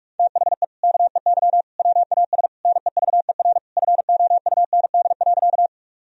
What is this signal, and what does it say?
Morse code, heard as a keyed tone, about 46 wpm, 700 Hz. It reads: THE KEY WAS NEVER FOUND.
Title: Sentences from Top 1000 Words - Review Part 4 - 45wpm